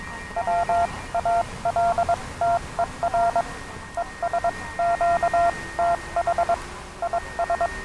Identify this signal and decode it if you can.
Morse code, heard as a keyed tone, about 22 wpm, spaced as for 19 wpm, 690 Hz. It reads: WALTER ES QTH IS
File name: attachments / Gromkaja STN na TPP